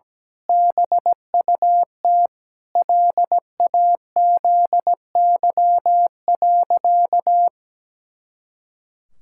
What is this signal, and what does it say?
Morse code, heard as a keyed tone, about 17 wpm, 700 Hz. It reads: BUT LAZY.
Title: Pangram Challenge v2 - 17wpm